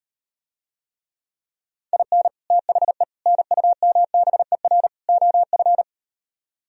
Morse code, heard as a keyed tone, about 38 wpm, 700 Hz. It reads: IN THE NUMBER OF